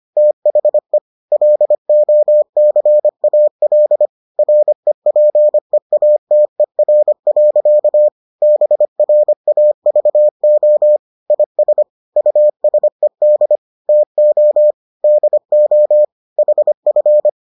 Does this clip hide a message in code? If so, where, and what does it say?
Yes: Morse code from 0.2 to 17.3 s, heard as a keyed tone, about 25 wpm, 600 Hz. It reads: THE LOCAL REPEATER. BRAVO IS USED TO DO HF